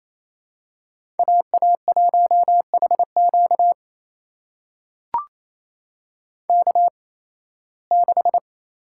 Morse code, heard as a keyed tone, about 28 wpm, spaced as for 18 wpm, 700 Hz. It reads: AA1HQ K6
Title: USA Call Signs - 5 Characters - 8x Character Spacing - 28wpm